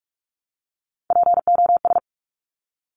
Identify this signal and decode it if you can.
Morse code, heard as a keyed tone, about 46 wpm, 700 Hz. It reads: POS
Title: Top Three Letter Combinations in Words 45wpm